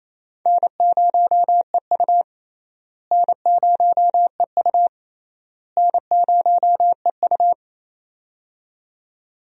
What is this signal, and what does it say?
Morse code, heard as a keyed tone, about 28 wpm, 700 Hz. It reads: N0EU N0EU N0EU
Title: USA Call Signs - 4 Characters - Repeated 3x - 28wpm